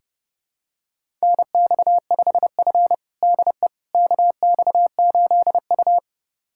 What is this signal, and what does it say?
Morse code, heard as a keyed tone, about 30 wpm, 700 Hz. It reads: NX5F DE KX8U
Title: USA Callsign DE Callsign - 4 Characters - 30wpm